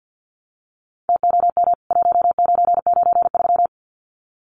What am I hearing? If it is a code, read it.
Morse code, heard as a keyed tone, about 49 wpm, 700 Hz. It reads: TOM 1993
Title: CWT Exchange 50wpm